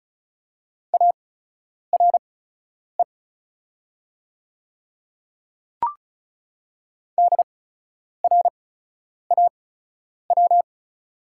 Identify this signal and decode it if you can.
Morse code, heard as a keyed tone, about 34 wpm, spaced as for 10 wpm, 700 Hz. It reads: ARE DRAW
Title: 8x Character Spacing - Top 300 Words - 35wpm